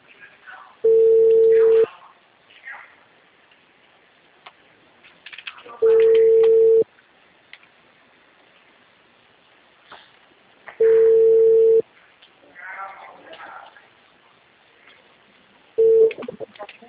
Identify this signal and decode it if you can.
Ringback tone (China, 450 Hz), with speech.